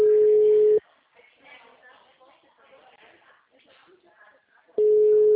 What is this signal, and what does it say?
Ringback tone (Europe, 425 Hz), with speech.